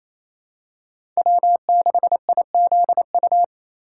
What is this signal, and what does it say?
Morse code, heard as a keyed tone, about 28 wpm, 700 Hz. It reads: W6IZU
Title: USA Call Signs - 5 Characters - 2x Character Spacing - 28wpm